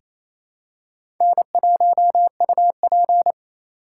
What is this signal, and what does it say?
Morse code, heard as a keyed tone, about 27 wpm, 700 Hz. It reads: N1UP
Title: USA Call Signs - 4 Characters - 3x Character Spacing - 28wpm